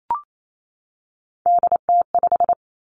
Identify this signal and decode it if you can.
Morse code, heard as a keyed tone, about 28 wpm, 700 Hz. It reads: DT5
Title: Three Challenging Letter-Number Combinations 28wpm